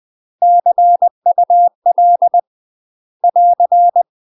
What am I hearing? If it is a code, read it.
Morse code, heard as a keyed tone, about 20 wpm, 700 Hz. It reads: CUL <AR>
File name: UK MASTER QSOs TRAINING / 20WPM / UK MASTER QSOs 0004